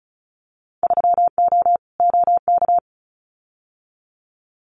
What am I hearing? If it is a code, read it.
Morse code, heard as a keyed tone, about 36 wpm, 700 Hz. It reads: 3O OK
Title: Winter Field Day Exchanges Encore 35wpm